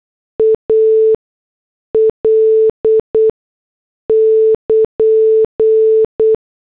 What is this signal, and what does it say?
Morse code, heard as a keyed tone, about 8 wpm, spaced as for 6 wpm, 440 Hz. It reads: AL <KN>